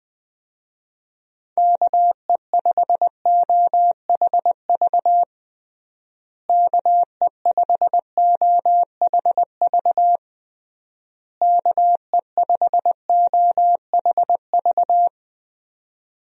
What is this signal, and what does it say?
Morse code, heard as a keyed tone, about 20 wpm, 700 Hz. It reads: KE5OHV KE5OHV KE5OHV
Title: USA Call Signs - 6 Characters - Repeated 3x - 20wpm